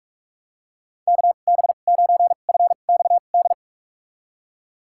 Morse code, heard as a keyed tone, about 47 wpm, spaced as for 33 wpm, 700 Hz. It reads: KB9FXD
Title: USA Call Signs - 6 Characters - 2x Character Spacing - 45wpm